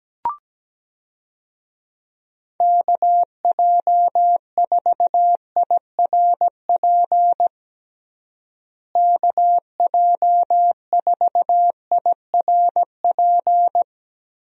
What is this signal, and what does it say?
Morse code, heard as a keyed tone, about 17 wpm, 700 Hz. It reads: KJ4IRP KJ4IRP
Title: USA Call Signs - 6 Characters - Repeated 3x - 17wpm